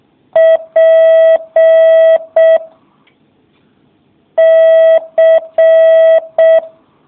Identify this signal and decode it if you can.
Morse code, heard as a keyed tone, about 6 wpm, 650 Hz. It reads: P C